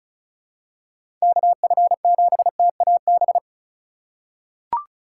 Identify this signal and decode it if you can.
Morse code, heard as a keyed tone, about 36 wpm, 700 Hz. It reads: KF7TAB
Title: USA Call Signs - 6 Characters - 3x Character Spacing - 35wpm